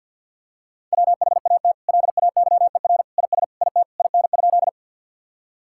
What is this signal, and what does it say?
Morse code, heard as a keyed tone, about 50 wpm, 700 Hz. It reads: WHAT LAYER IS IT IN?
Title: Sentences from Top 1600 Words 50wpm